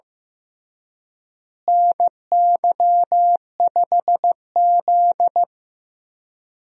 Morse code, heard as a keyed tone, about 15 wpm, 700 Hz. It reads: NY5Z
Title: USA Call Signs - 4 Characters - Repeated 3x - 15wpm